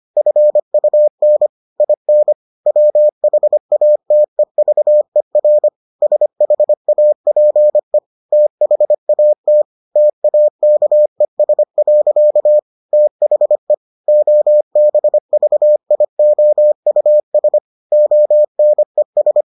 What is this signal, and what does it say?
Morse code, heard as a keyed tone, about 25 wpm, 600 Hz. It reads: FUN IN WHATEVER SHAPE THAT TAKES. THE OBVIOUS ONES